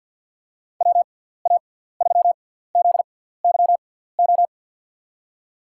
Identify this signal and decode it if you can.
Morse code, heard as a keyed tone, about 53 wpm, spaced as for 17 wpm, 700 Hz. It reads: WA3ZYO
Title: USA Call Signs - 6x Character Spacing - 50wpm